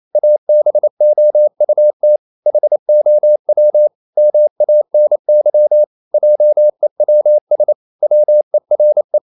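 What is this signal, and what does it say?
Morse code, heard as a keyed tone, about 28 wpm, 600 Hz. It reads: ABOUT HOW MANY JEWS WERE
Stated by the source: Morse code practice at 28 wpm, text of Ken Follett's novel Never in English, from CW med Gnister.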